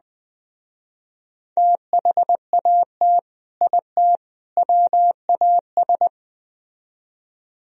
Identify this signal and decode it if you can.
Morse code, heard as a keyed tone, about 21 wpm, 700 Hz. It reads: THAT IT WAS
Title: Speed-Racing - Sets of 3 Words 17wpm